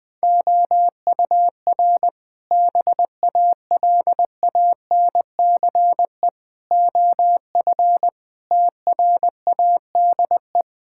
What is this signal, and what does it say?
Morse code, heard as a keyed tone, about 21 wpm, 700 Hz. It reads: OUR BALANCE OF TRADE
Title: Sentences from Top 1000 Words - Review Part 6 - 20wpm